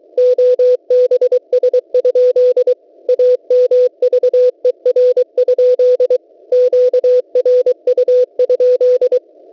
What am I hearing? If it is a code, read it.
Morse code, heard as a keyed tone, about 23 wpm, 500 Hz. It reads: OBS? AMVER? QRU?
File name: WebSDR records / w7pua-2.ddns.net 8642.00 cw